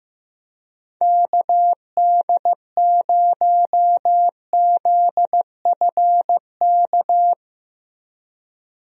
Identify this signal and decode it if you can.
Morse code, heard as a keyed tone, about 15 wpm, 700 Hz. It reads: KD0ZFK
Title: USA Call Signs - 6 Characters - Repeated 3x - 15wpm